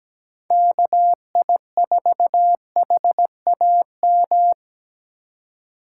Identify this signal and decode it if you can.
Morse code, heard as a keyed tone, about 17 wpm, 700 Hz. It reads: KI4HAM
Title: USA Call Signs - 6 Characters - 3x Character Spacing - 17wpm